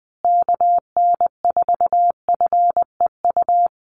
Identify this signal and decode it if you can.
Morse code, heard as a keyed tone, about 20 wpm, 700 Hz. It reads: KN4FEU
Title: USA Call Signs Repeated 2x 20wpm